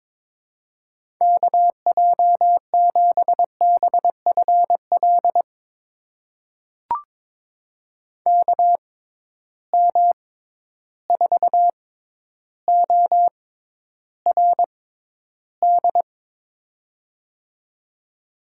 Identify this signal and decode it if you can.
Morse code, heard as a keyed tone, about 22 wpm, spaced as for 11 wpm, 700 Hz. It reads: KJ7BFL KM4ORD
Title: USA Call Signs - 6 Characters - 6x Character Spacing - 22wpm